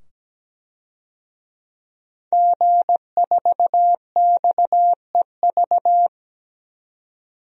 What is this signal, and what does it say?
Morse code, heard as a keyed tone, about 17 wpm, 700 Hz. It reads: G4XEV
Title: DXCC-100 Call Signs - 5 Characters - 17wpm